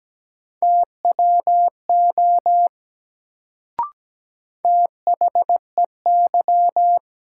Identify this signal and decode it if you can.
Morse code, heard as a keyed tone, about 17 wpm, 700 Hz. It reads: TWO THEY
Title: Sets of 2 Words - New Word Drill - 17wpm